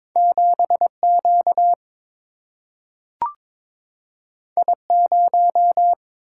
Morse code, heard as a keyed tone, about 22 wpm, 700 Hz. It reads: 7Q I0
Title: Single-Letter Single-Number 22wpm